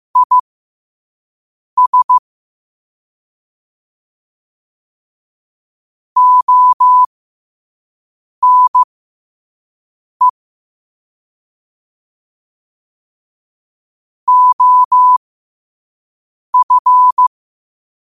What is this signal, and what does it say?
Morse code, heard as a keyed tone, about 15 wpm, spaced as for 5 wpm, 1000 Hz. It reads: IS ONE OF
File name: qotdfiles / 5wpm